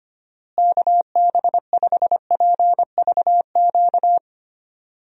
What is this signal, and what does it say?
Morse code, heard as a keyed tone, about 25 wpm, 700 Hz. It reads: KB5PVQ
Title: USA Call Signs - 6 Characters - Repeated 3x - 25wpm